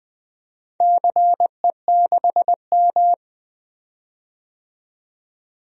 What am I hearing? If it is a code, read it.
Morse code, heard as a keyed tone, about 20 wpm, 700 Hz. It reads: CE6M
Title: DXCC-100 Call Signs - 4 Characters - 20wpm